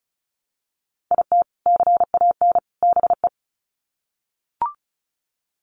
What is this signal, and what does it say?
Morse code, heard as a keyed tone, about 34 wpm, 700 Hz. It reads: IT CAN BE
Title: Sets of 3 Words 35wpm